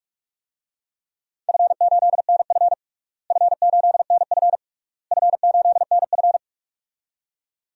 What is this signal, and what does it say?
Morse code, heard as a keyed tone, about 41 wpm, 700 Hz. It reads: F8NF F8NF F8NF
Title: DXCC-100 Call Signs - 4 Characters - Repeated 3x - 45wpm